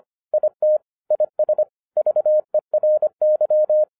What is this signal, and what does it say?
Morse code, heard as a keyed tone, about 25 wpm, 610 Hz. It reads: IT IS VERY
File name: 2011-10-mp3 / n09-CW TO PS2 CONVERTER 25wpm